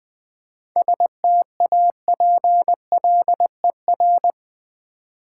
Morse code, heard as a keyed tone, about 20 wpm, 700 Hz. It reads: STAPLER
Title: Easy Words with Hints - 20wpm